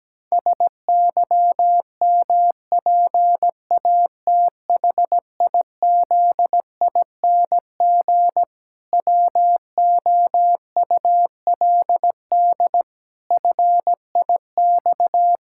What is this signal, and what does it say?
Morse code, heard as a keyed tone, about 17 wpm, 700 Hz. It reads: SYMPATHIZING WOULD FIX
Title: Pangram Challenge v2 - 17wpm